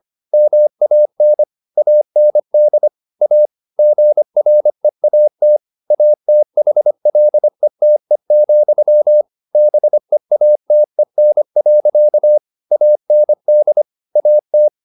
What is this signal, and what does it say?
Morse code, heard as a keyed tone, about 25 wpm, 600 Hz. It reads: MAN AND A GREAT ATHLETE, BEATEN. AND AT